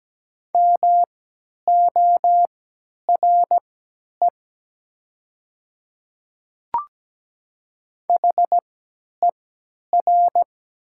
Morse code, heard as a keyed tone, about 17 wpm, spaced as for 10 wpm, 700 Hz. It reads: MORE HER
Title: 3x Character Spacing - Top 300 Words - 17wpm